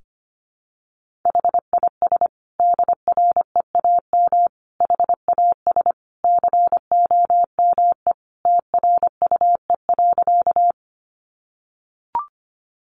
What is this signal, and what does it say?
Morse code, heard as a keyed tone, about 25 wpm, 700 Hz. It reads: HIS DREAM HAS COME TRUE.
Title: Sentences from Top 1500 Words 25wpm